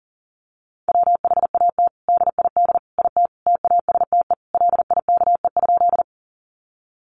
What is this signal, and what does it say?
Morse code, heard as a keyed tone, about 40 wpm, 700 Hz. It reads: WHAT DID IT TASTE LIKE?
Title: Sentences from Top 1100 Words 40wpm